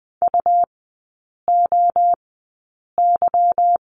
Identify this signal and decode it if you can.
Morse code, heard as a keyed tone, about 20 wpm, spaced as for 8 wpm, 690 Hz. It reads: UOY
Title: 145 Lesson 24e - Letter y - Three Characters 20wpm